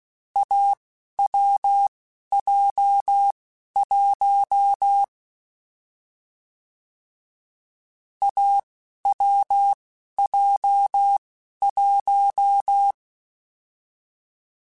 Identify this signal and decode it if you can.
Morse code, heard as a keyed tone, about 16 wpm, spaced as for 11 wpm, 800 Hz. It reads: AWJ1 AWJ1
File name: Difficult Combos / Difficult Combos-mp3s / awj1 ndb6 gz7 @10